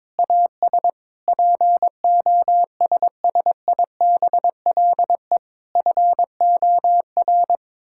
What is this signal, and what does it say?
Morse code, heard as a keyed tone, about 22 wpm, 700 Hz. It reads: AS POSSIBLE FOR
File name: practice / 22wpm / Sets of 8 Words - 22wpm